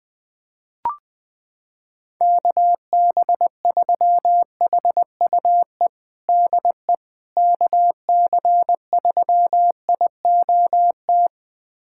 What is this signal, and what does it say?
Morse code, heard as a keyed tone, about 20 wpm, 700 Hz. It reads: KB3HUE DE KC3IOT
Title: USA Callsign DE Callsign - 6 Characters - 20wpm